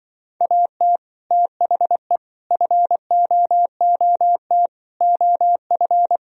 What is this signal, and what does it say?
Morse code, heard as a keyed tone, about 24 wpm, 700 Hz. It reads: AT THE FOOT OF
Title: Speed-Racing - Sets of 5 Words 20wpm